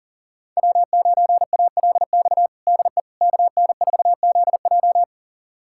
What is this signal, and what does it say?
Morse code, heard as a keyed tone, about 40 wpm, 700 Hz. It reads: W9ALX DE KN4ZJ
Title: USA Callsign DE Callsign - 5 Characters - 40wpm